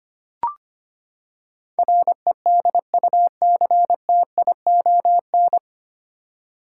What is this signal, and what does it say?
Morse code, heard as a keyed tone, about 25 wpm, 700 Hz. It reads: REDUCTION
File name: Nine Letter Words - 25wpm